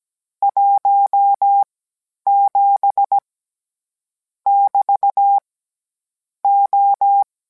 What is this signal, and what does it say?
Morse code, heard as a keyed tone, about 17 wpm, spaced as for 10 wpm, 800 Hz. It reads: 17 <BT> O